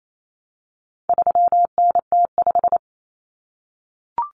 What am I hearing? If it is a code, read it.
Morse code, heard as a keyed tone, about 28 wpm, 700 Hz. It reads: 3NT5
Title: Common Cut Numbers - 4-Digit 28wpm